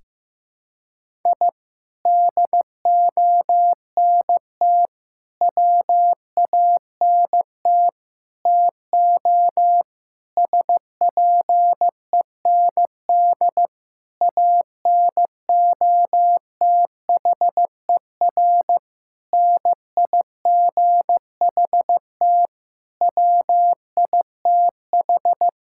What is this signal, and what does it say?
Morse code, heard as a keyed tone, about 15 wpm, 700 Hz. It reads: I DONT WANT TO SPEND ANOTHER NIGHT WITH
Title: Sentences from Top 1000 Words - Review Part 5 - 15wpm